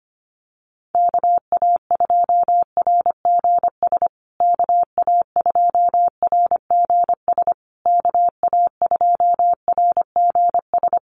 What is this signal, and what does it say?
Morse code, heard as a keyed tone, about 25 wpm, 700 Hz. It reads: KA2RGS KA2RGS KA2RGS
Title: USA Call Signs Repeated 3x 25wpm